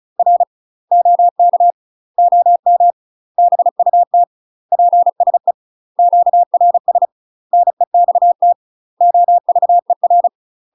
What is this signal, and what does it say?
Morse code, heard as a keyed tone, about 33 wpm, 700 Hz. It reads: R OK OM BUT PSE QRS NEXT OVER
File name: UK MASTER QSOs TRAINING / 35WPM / UK MASTER QSOs 0002